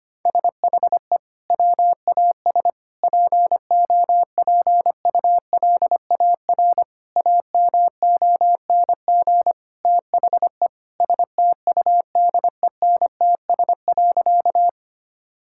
Morse code, heard as a keyed tone, about 25 wpm, 700 Hz. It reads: SHE WAS POPULAR AMONG THE STUDENTS.
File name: practice / 25wpm / Sentences with 2-Word Phrases - 25wpm